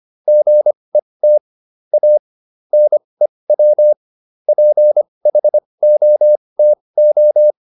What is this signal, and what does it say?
Morse code, heard as a keyed tone, about 25 wpm, spaced as for 20 wpm, 600 Hz. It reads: GET A NEW PHOTO